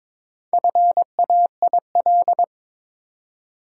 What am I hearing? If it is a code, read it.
Morse code, heard as a keyed tone, about 23 wpm, 700 Hz. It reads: FAIL
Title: Four Letter Words - 22wpm